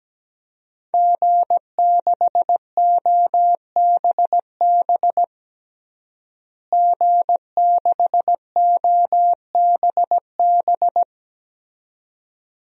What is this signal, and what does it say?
Morse code, heard as a keyed tone, about 17 wpm, 700 Hz. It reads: G6OBB G6OBB
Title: DXCC-100 Call Signs - 5 Characters - Repeated 2x - 17wpm